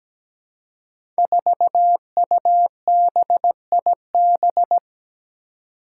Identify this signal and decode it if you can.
Morse code, heard as a keyed tone, about 17 wpm, 700 Hz. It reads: 4UBIB
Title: Five Challenging Letter-Number Combinations 17wpm